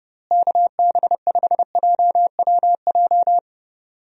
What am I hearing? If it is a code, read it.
Morse code, heard as a keyed tone, about 30 wpm, 700 Hz. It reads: KB5JWJ
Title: USA Call Signs - 6 Characters - Repeated 2x - 30wpm